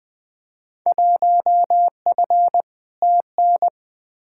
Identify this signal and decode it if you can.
Morse code, heard as a keyed tone, about 20 wpm, 700 Hz. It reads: 1F TN
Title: Field Day Exchanges - 20wpm